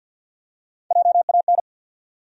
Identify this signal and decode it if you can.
Morse code, heard as a keyed tone, about 53 wpm, 700 Hz. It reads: JAN
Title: USA Common Names - 50wpm